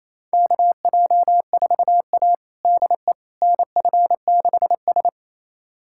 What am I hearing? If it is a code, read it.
Morse code, heard as a keyed tone, about 27 wpm, 700 Hz. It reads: KJ4A DE NF6S
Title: USA Callsign DE Callsign - 4 Characters - 28wpm